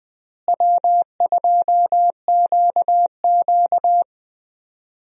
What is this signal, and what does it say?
Morse code, heard as a keyed tone, about 20 wpm, 700 Hz. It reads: W2QQ
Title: USA Call Signs - 4 Characters - Repeated 3x - 20wpm